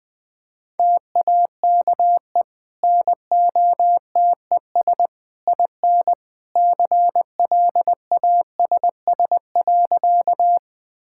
Morse code, heard as a keyed tone, about 21 wpm, 700 Hz. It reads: TAKE NOTES IN CLASS.
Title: Easy Sentences with Hints - 20wpm